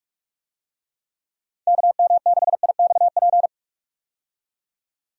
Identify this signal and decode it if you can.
Morse code, heard as a keyed tone, about 46 wpm, 700 Hz. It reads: KM6IXP
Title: USA Call Signs - 6 Characters - 45wpm